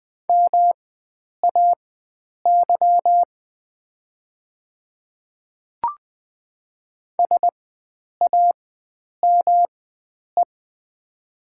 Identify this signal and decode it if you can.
Morse code, heard as a keyed tone, about 20 wpm, spaced as for 9 wpm, 700 Hz. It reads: MAY SAME
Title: 4x Character Spacing - Top 300 Words - 20wpm